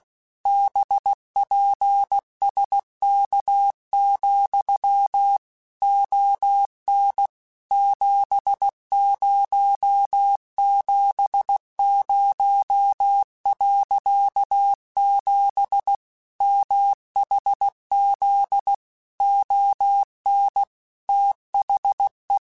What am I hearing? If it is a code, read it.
Morse code, heard as a keyed tone, about 16 wpm, 800 Hz. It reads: BPSK, ON 7070.7 MHZ ON THE
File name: QSOs / QSOs-mp3s / kc7haa@15